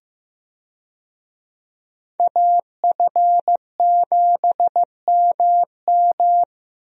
Morse code, heard as a keyed tone, about 15 wpm, 700 Hz. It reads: AF7MM
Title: USA Call Signs - 5 Characters - Repeated 2x - 15wpm